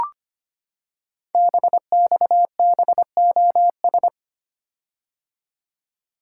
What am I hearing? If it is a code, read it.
Morse code, heard as a keyed tone, about 25 wpm, 700 Hz. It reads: BXBOS
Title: Code Groups - 5 Letters - 25wpm